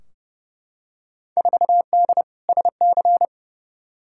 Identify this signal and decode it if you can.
Morse code, heard as a keyed tone, about 32 wpm, 700 Hz. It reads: 4D SC
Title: Field Day Exchanges Encore - 30wpm